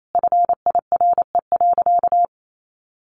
Morse code, heard as a keyed tone, about 28 wpm, 700 Hz. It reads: FIRE.
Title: Sentences from Top 1400 Words 28wpm